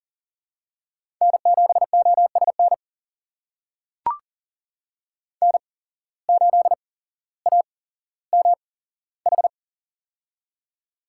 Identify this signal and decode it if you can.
Morse code, heard as a keyed tone, about 38 wpm, spaced as for 17 wpm, 700 Hz. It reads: N7OSN N8AMH
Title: USA Call Signs - 5 Characters - 8x Character Spacing - 40wpm